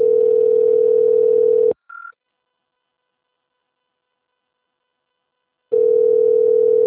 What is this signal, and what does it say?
Ringback tone (North America), 440 + 480 Hz.
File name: out- - - - - .1573